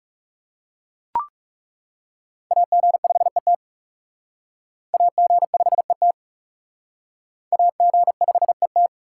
Morse code, heard as a keyed tone, about 37 wpm, 700 Hz. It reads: AG5ET AG5ET AG5ET
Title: Speed Racing - USA 2x2 Call Signs 30wpm